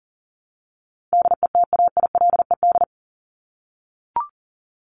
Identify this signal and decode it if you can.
Morse code, heard as a keyed tone, about 40 wpm, 700 Hz. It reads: DETAILED
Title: Top 1251-1500 Words 40wpm